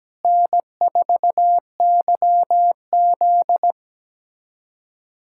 Morse code, heard as a keyed tone, about 17 wpm, 700 Hz. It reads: N4YZ
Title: USA Callsign DE Callsign - 4 Characters - 17wpm